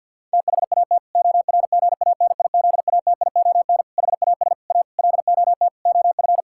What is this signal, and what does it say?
Morse code, heard as a keyed tone, about 52 wpm, 700 Hz. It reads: THAT ORGANIZATION HAS A LOT OF
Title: Sentences from Top 1000 Words - Review Part 6 - 50wpm